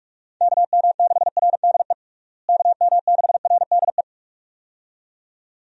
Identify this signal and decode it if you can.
Morse code, heard as a keyed tone, about 46 wpm, 700 Hz. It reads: KM6RDE KM6RDE
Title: USA Call Signs - 6 Characters - Repeated 2x - 45wpm